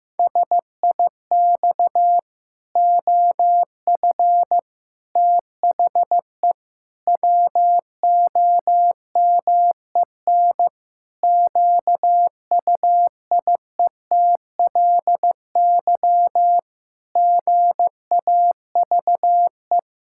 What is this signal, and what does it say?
Morse code, heard as a keyed tone, about 16 wpm, 700 Hz. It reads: SIX OF THE WOMEN QUIETLY GAVE
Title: Pangram Challenge v2 - 15wpm